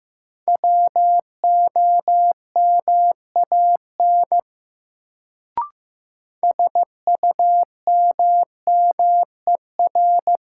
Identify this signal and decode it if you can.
Morse code, heard as a keyed tone, about 15 wpm, 700 Hz. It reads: WOMAN SUMMER
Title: Sentences from Top 1100 Words - New Word Drill Alt - 15wpm